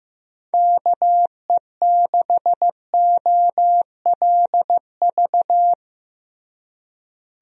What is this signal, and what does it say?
Morse code, heard as a keyed tone, about 15 wpm, 700 Hz. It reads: KE6OLV
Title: USA Call Signs - 6 Characters - Repeated 3x - 15wpm